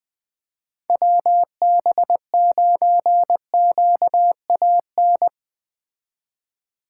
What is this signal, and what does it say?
Morse code, heard as a keyed tone, about 21 wpm, 700 Hz. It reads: WB9QAN
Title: Speed Racing - USA 2x3 Call Signs 17wpm